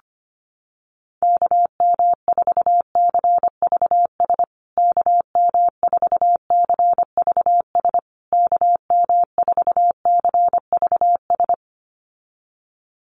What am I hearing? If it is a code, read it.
Morse code, heard as a keyed tone, about 25 wpm, 700 Hz. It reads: KM4CVS KM4CVS KM4CVS
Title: USA Call Signs Repeated 3x 25wpm